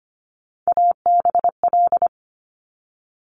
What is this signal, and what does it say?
Morse code, heard as a keyed tone, about 25 wpm, 700 Hz. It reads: ABL
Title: Top Three Letter Combinations in Words 25wpm